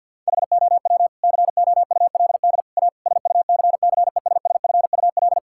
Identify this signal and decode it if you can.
Morse code, heard as a keyed tone, about 53 wpm, 700 Hz. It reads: HOW COULD A SUCCESSFUL
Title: Sentences from Top 1000 Words - Review Part 6 - 50wpm